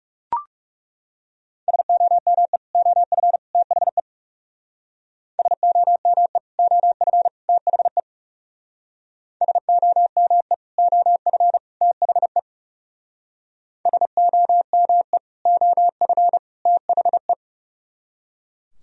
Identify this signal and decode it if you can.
Morse code, heard as a keyed tone, about 36 wpm, 700 Hz. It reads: SOME OF THE SOME OF THE SOME OF THE SOME OF THE
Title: Speed-Racing - Sets of 3 Words 30wpm